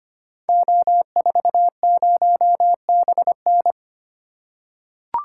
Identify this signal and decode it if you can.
Morse code, heard as a keyed tone, about 25 wpm, 700 Hz. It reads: O40BN
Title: Code Groups - 5 Letter:Numbers - 25wpm